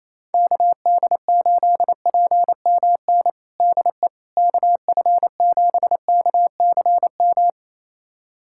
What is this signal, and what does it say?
Morse code, heard as a keyed tone, about 27 wpm, 700 Hz. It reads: KD8PMN DE KF7KCM
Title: USA Callsign DE Callsign - 6 Characters - 28wpm